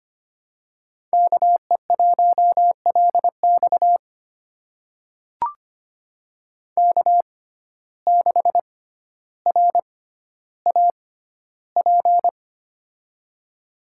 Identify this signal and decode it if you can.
Morse code, heard as a keyed tone, about 25 wpm, spaced as for 13 wpm, 700 Hz. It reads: KE1LX K6RAP
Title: USA Call Signs - 6x Character Spacing - 25wpm